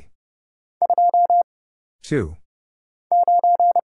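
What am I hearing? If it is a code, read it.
Morse code, heard as a keyed tone, about 30 wpm, 700 Hz. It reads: 2 9
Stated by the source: Morse Code Ninja single-character practice (letters, numbers, punctuation, prosigns) at 30 wpm, by Kurt Zoglmann (AD0WE).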